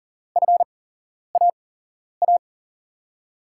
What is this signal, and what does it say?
Morse code, heard as a keyed tone, about 40 wpm, spaced as for 11 wpm, 700 Hz. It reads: FAA
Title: USA Call Signs - 5 Characters - 8x Character Spacing - 40wpm